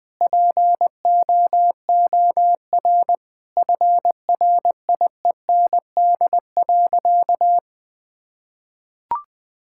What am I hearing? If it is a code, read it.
Morse code, heard as a keyed tone, about 21 wpm, 700 Hz. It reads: POOR FRIEND.
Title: Sentences from Top 1000 Words - Review Part 5 - 20wpm